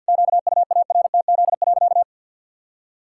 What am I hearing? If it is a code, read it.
Morse code, heard as a keyed tone, about 51 wpm, 700 Hz. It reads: QUARTZ.
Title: Pangram Challenge v2 - 50wpm